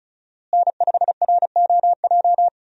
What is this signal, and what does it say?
Morse code, heard as a keyed tone, about 34 wpm, 700 Hz. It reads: N5ROJ
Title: USA Call Signs - 5 Characters - Repeated 2x - 35wpm